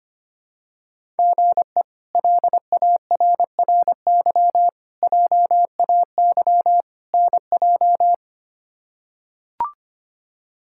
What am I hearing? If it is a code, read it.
Morse code, heard as a keyed tone, about 25 wpm, 700 Hz. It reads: GE LARRY JAY NJ
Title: K1USN SST QSO Practice - 25wpm